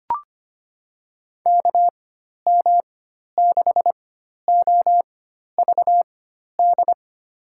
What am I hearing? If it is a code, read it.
Morse code, heard as a keyed tone, about 25 wpm, spaced as for 12 wpm, 700 Hz. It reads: KM6OVD
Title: USA Call Signs - 6 Characters - 4x Character Spacing - 25wpm